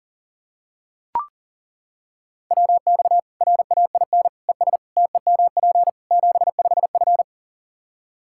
Morse code, heard as a keyed tone, about 40 wpm, 700 Hz. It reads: WX RAIN ES TEMP 75F
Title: Practice QSO - Exchange 2 - Info - 40wpm